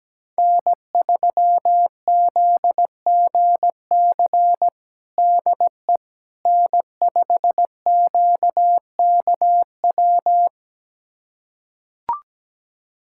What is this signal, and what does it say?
Morse code, heard as a keyed tone, about 17 wpm, 700 Hz. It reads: N3ZGC DE N5QKW
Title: USA Callsign DE Callsign - 5 Characters - 17wpm